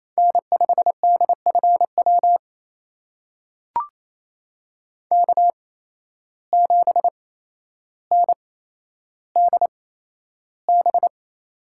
Morse code, heard as a keyed tone, about 27 wpm, spaced as for 12 wpm, 700 Hz. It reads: N5DFW K7NDB
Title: USA Call Signs - 5 Characters - 8x Character Spacing - 28wpm